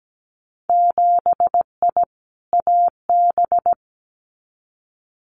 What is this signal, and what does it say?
Morse code, heard as a keyed tone, about 17 wpm, 700 Hz. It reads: 7I AB
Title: Winter Field Day Exchanges Encore 17wpm